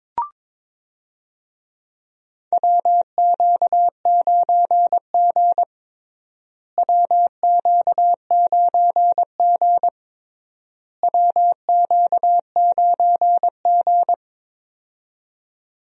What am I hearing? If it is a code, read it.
Morse code, heard as a keyed tone, about 22 wpm, 700 Hz. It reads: WQ9G WQ9G WQ9G
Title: USA Call Signs - 4 Characters - Repeated 3x - 22wpm